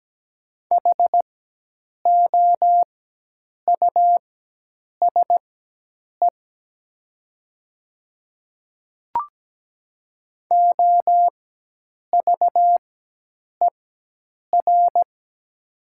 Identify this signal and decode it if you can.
Morse code, heard as a keyed tone, about 17 wpm, spaced as for 8 wpm, 700 Hz. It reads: HOUSE OVER
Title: 4x Character Spacing - Top 300 Words - 17wpm